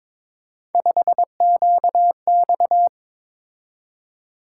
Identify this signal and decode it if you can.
Morse code, heard as a keyed tone, about 22 wpm, 700 Hz. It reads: 5QX
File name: practice / 22wpm / Three Letter-Number Combinations - 22wpm